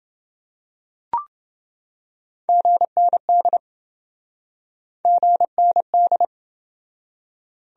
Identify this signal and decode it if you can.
Morse code, heard as a keyed tone, about 27 wpm, 700 Hz. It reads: GND GND